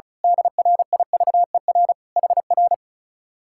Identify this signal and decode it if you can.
Morse code, heard as a keyed tone, about 34 wpm, 700 Hz. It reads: DRIVER HR
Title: Speed-Racing Common QSO Phrases - 35wpm